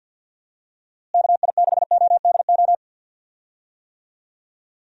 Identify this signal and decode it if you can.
Morse code, heard as a keyed tone, about 47 wpm, 700 Hz. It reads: KI6ODO
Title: USA Call Signs - 6 Characters - Repeated 3x - 50wpm